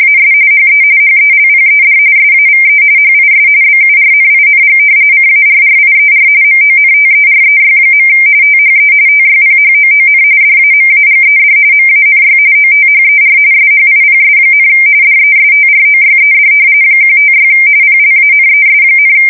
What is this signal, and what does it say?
RTTY, about 45.45 baud. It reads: AS HE HELPED SAVE HIS BURNING CARRIER MONTEREY. THE TYPHOON DAMAGED MANY OTHER SHIPS, DESTROYED 156 PLANES,
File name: fsk170 / battleshipnj